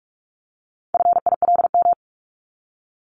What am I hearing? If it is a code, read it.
Morse code, heard as a keyed tone, about 44 wpm, 700 Hz. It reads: FILM
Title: Top 501-750 Words 45wpm